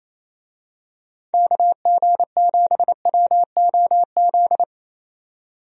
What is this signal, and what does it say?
Morse code, heard as a keyed tone, about 28 wpm, 700 Hz. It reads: KG7WOZ
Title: USA Call Signs - 6 Characters - 28wpm